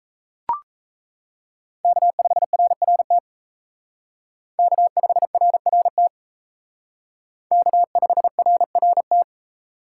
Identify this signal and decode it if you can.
Morse code, heard as a keyed tone, about 34 wpm, 700 Hz. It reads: K5RRT K5RRT K5RRT
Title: Speed Racing - USA 1x3 Call Signs 28wpm